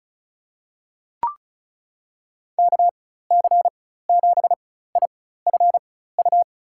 Morse code, heard as a keyed tone, about 36 wpm, spaced as for 16 wpm, 700 Hz. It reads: KC7IFU
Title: USA Call Signs - 6 Characters - 4x Character Spacing - 35wpm